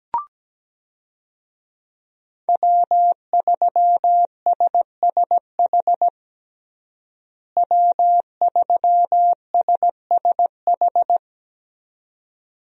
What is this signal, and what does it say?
Morse code, heard as a keyed tone, about 17 wpm, 700 Hz. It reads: W3SSH W3SSH
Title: USA Call Signs - 5 Characters - Repeated 2x - 17wpm